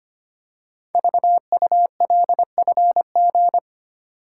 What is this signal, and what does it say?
Morse code, heard as a keyed tone, about 25 wpm, 700 Hz. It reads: VULFG
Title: Code Groups - 5 Letters - 25wpm